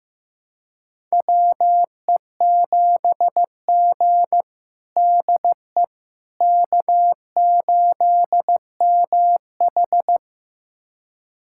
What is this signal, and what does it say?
Morse code, heard as a keyed tone, about 15 wpm, 700 Hz. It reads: WE7G DE K8MH
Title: USA Callsign DE Callsign - 4 Characters - 15wpm